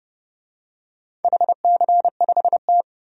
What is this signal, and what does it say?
Morse code, heard as a keyed tone, about 30 wpm, 700 Hz. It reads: HC5T
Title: DXCC-100 Call Signs - 4 Characters - 30wpm